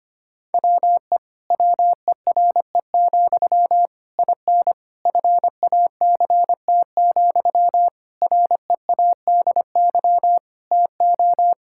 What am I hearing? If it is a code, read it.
Morse code, heard as a keyed tone, about 25 wpm, 700 Hz. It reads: WE WERE, IN FACT, READY TO
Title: Sentences from Top 1000 Words - Review Part 5 - 25wpm